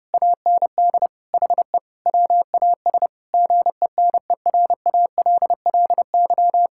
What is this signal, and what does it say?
Morse code, heard as a keyed tone, about 31 wpm, 700 Hz. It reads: AND HE WAS GENERALLY
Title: Sentences from Top 1000 Words - Review Part 6 - 30wpm